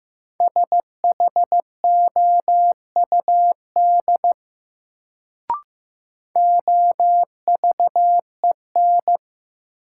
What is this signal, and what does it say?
Morse code, heard as a keyed tone, about 15 wpm, 700 Hz. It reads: SHOUD OVEN
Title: Sentences from Top 1000 Words - Review Part 4 - New Word Drill - 15wpm